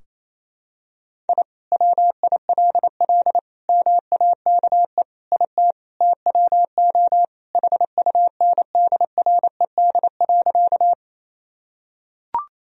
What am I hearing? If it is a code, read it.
Morse code, heard as a keyed tone, about 28 wpm, 700 Hz. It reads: I WILL MAKE IT TWO HUNDRED.
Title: Sentences from Top 1000 Words - Review Part 2 - 28wpm